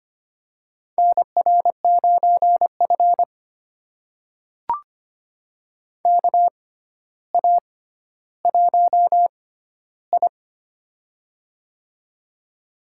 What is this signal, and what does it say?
Morse code, heard as a keyed tone, about 25 wpm, spaced as for 13 wpm, 700 Hz. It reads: NR9F KA1I
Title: USA Call Signs - 4 Characters - 6x Character Spacing - 25wpm